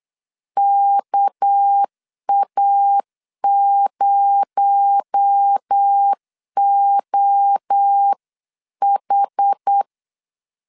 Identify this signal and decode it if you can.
Morse code, heard as a keyed tone, about 8 wpm, 800 Hz. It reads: KA0OH
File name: CW / DET / 1 6 1500C S B 8wpm 5142